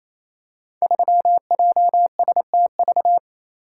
Morse code, heard as a keyed tone, about 27 wpm, 700 Hz. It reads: 3JSTV
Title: Code Groups - 5 Letter:Numbers - 28wpm